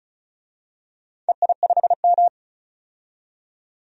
Morse code, heard as a keyed tone, about 34 wpm, 700 Hz. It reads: EI5M
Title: DXCC-100 Call Signs - 4 Characters - 35wpm